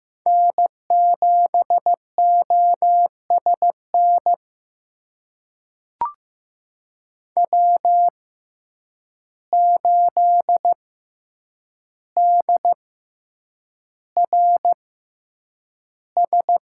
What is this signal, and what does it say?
Morse code, heard as a keyed tone, about 15 wpm, spaced as for 8 wpm, 700 Hz. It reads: N7OSN W8DRS
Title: USA Call Signs - 5 Characters - 6x Character Spacing - 15wpm